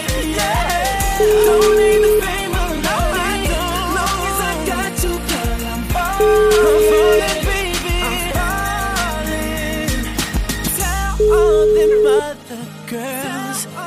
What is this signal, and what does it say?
Ringback tone (Europe, 425 Hz), with noise.